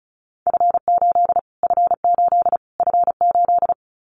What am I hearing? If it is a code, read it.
Morse code, heard as a keyed tone, about 34 wpm, 700 Hz. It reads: F8 F8 F8
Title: Callsign Prefixes for Top 25 DXCC - Repeated 3x - 35wpm